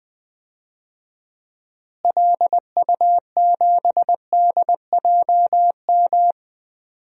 Morse code, heard as a keyed tone, about 20 wpm, 700 Hz. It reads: LU7DJM
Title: DXCC-100 Call Signs - 6 Characters - Repeated 2x - 20wpm